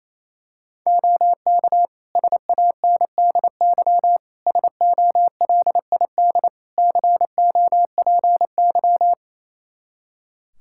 Morse code, heard as a keyed tone, about 28 wpm, 700 Hz. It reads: OK SANDY SOLID COPY
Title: Practice QSO - Exchange 3 - More Info - 28wpm